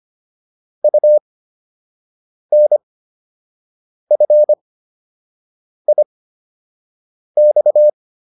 Morse code, heard as a keyed tone, about 25 wpm, spaced as for 6 wpm, 600 Hz. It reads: UNFIX
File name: UN5 06